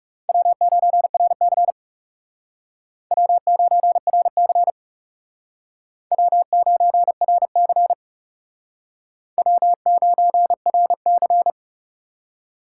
Morse code, heard as a keyed tone, about 36 wpm, 700 Hz. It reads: W9RC W9RC W9RC W9RC
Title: Speed Racing - USA 1x2 Call Signs 30wpm